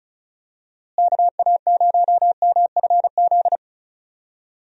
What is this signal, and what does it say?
Morse code, heard as a keyed tone, about 36 wpm, 700 Hz. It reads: KA0MFZ
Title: USA Call Signs - 6 Characters - Repeated 3x - 35wpm